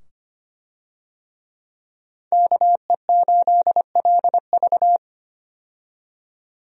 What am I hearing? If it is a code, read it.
Morse code, heard as a keyed tone, about 25 wpm, 700 Hz. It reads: KE8LV
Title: USA Call Signs - 5 Characters - 25wpm